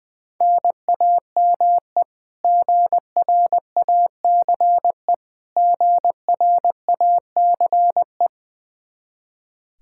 Morse code, heard as a keyed tone, about 20 wpm, 700 Hz. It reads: NAME GRACE GRACE
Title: Practice QSO - Exchange 1 - Beginning - 20wpm